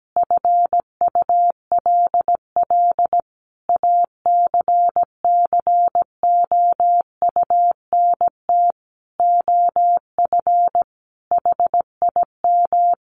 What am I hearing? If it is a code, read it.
Morse code, heard as a keyed tone, about 17 wpm, 700 Hz. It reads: FULL ACCOUNT OF HIM
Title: Sentences from Top 900 Words 17wpm